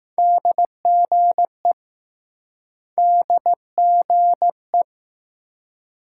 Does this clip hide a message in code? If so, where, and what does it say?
Yes: Morse code from 0.2 to 4.8 s, heard as a keyed tone, about 20 wpm, 700 Hz. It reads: DGE DGE